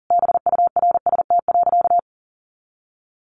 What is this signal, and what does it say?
Morse code, heard as a keyed tone, about 40 wpm, 700 Hz. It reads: BURST.
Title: A Sound Of Thunder 40wpm